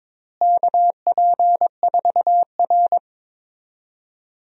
Morse code, heard as a keyed tone, about 22 wpm, 700 Hz. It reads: KP4R
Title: DXCC-100 Call Signs - 4 Characters - Repeated 3x - 22wpm